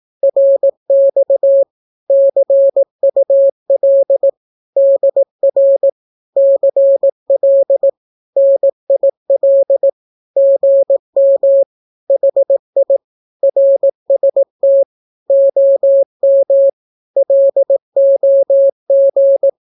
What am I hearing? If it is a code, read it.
Morse code, heard as a keyed tone, about 18 wpm, 550 Hz. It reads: RX CUL DR CL NIL GM HI RST OM LOG
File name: naukapodstaw / skroty / skroty